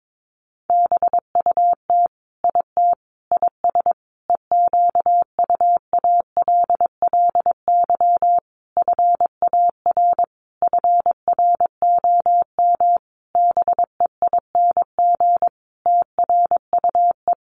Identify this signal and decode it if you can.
Morse code, heard as a keyed tone, about 22 wpm, 700 Hz. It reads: BUT IT IS EQUALLY FAR FROM BEING TRUE
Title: Sentences from Top 1400 Words 22wpm